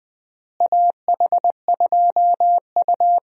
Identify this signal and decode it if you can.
Morse code, heard as a keyed tone, about 21 wpm, 700 Hz. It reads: AH2U
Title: USA Call Signs - 4 Characters - Repeated 3x - 20wpm